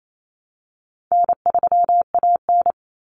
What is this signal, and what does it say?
Morse code, heard as a keyed tone, about 28 wpm, 700 Hz. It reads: N3AN
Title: Common Cut Numbers - 4-Digit 28wpm